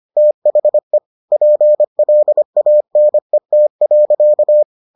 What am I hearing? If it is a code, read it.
Morse code, heard as a keyed tone, about 25 wpm, 600 Hz. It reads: THE PLANET.